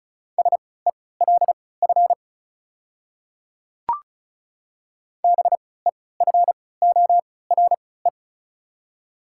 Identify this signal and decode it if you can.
Morse code, heard as a keyed tone, about 36 wpm, spaced as for 20 wpm, 690 Hz. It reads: SELF BEFORE
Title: 3x Character Spacing - Top 300 Words - 35wpm